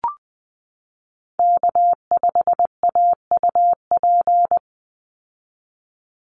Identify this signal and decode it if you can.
Morse code, heard as a keyed tone, about 20 wpm, 700 Hz. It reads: K5AUP
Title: Call Signs Repeated 3x - 20wpm